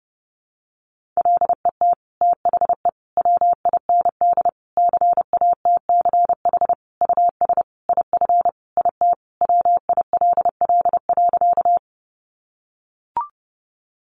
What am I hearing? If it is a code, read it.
Morse code, heard as a keyed tone, about 28 wpm, 700 Hz. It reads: LET THE WIND CATCH US IF IT WILL.
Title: Sentences from Top 1500 Words 30wpm